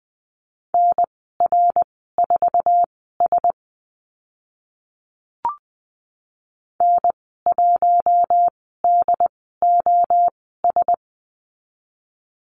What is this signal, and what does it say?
Morse code, heard as a keyed tone, about 20 wpm, spaced as for 15 wpm, 700 Hz. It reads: NR4S N1DOS
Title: 082 Lesson 14g - Number 2 - Call Signs 20wpm